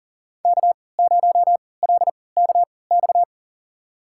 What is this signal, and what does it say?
Morse code, heard as a keyed tone, about 40 wpm, spaced as for 23 wpm, 700 Hz. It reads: K0LKX
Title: USA Call Signs - 5 Characters - 3x Character Spacing - 40wpm